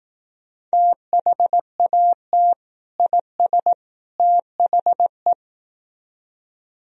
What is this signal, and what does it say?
Morse code, heard as a keyed tone, about 18 wpm, 700 Hz. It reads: THAT IS THE
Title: Speed-Racing - Sets of 3 Words 15wpm